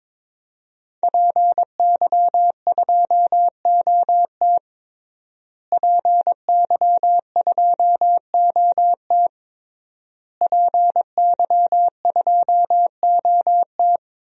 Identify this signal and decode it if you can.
Morse code, heard as a keyed tone, about 22 wpm, 700 Hz. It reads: PY2OT PY2OT PY2OT
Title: DXCC-100 Call Signs - 5 Characters - Repeated 3x - 22wpm